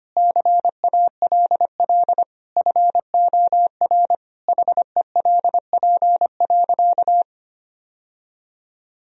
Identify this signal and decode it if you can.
Morse code, heard as a keyed tone, about 25 wpm, 700 Hz. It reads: CALL FOR HELP.
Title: Sentences from Top 1000 Words - Review Part 4 - 25wpm